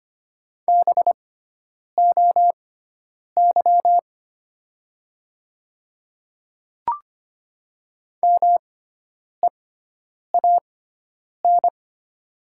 Morse code, heard as a keyed tone, about 25 wpm, spaced as for 9 wpm, 700 Hz. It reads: BOY MEAN
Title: 6x Character Spacing - Top 200 Words - 25wpm